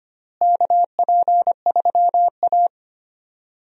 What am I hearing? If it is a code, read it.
Morse code, heard as a keyed tone, about 25 wpm, 700 Hz. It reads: KP3A
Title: USA Call Signs - 4 Characters - Repeated 2x - 25wpm